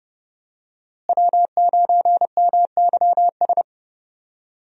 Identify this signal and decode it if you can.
Morse code, heard as a keyed tone, about 30 wpm, 700 Hz. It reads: W9MYS
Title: USA Call Signs - 5 Characters - Repeated 2x - 30wpm